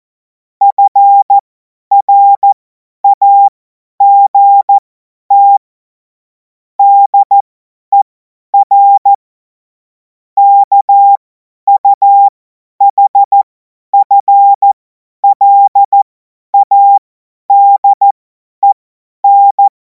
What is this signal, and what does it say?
Morse code, heard as a keyed tone, about 14 wpm, spaced as for 10 wpm, 800 Hz. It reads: FRAGT DER KUHFLADEN
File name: Morsekurs / Witze / Witz 21. April 2026 50 F70